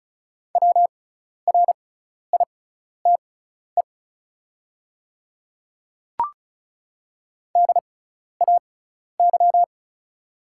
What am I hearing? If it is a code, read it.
Morse code, heard as a keyed tone, about 36 wpm, spaced as for 12 wpm, 690 Hz. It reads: WRITE DAY
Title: 6x Character Spacing - Top 100 Words - 35wpm